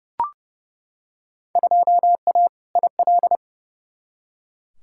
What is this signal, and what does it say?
Morse code, heard as a keyed tone, about 29 wpm, 700 Hz. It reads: 2A IL
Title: Field Day Exchanges - 30wpm